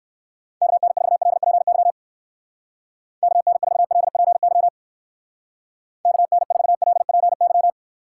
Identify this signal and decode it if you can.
Morse code, heard as a keyed tone, about 58 wpm, 700 Hz. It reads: KN4LPY KN4LPY KN4LPY